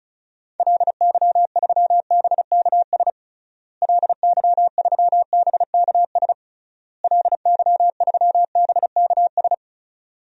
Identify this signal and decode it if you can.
Morse code, heard as a keyed tone, about 36 wpm, 700 Hz. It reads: LY3BKS LY3BKS LY3BKS
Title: DXCC-100 Call Signs - 6 Characters - Repeated 3x - 35wpm